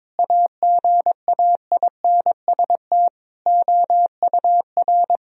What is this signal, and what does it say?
Morse code, heard as a keyed tone, about 22 wpm, 700 Hz. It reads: AGAINST OUR
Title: Sentences from Top 1000 Words - Review Part 4 - 22wpm